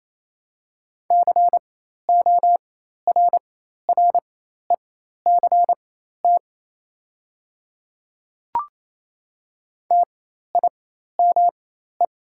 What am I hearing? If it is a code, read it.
Morse code, heard as a keyed tone, about 28 wpm, spaced as for 13 wpm, 700 Hz. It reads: CORRECT TIME
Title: 4x Character Spacing - Top 500 Words - 28wpm